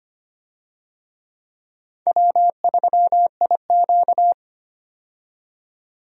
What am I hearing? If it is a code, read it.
Morse code, heard as a keyed tone, about 25 wpm, 700 Hz. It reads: W3IQ
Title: USA Call Signs - 4 Characters - 25wpm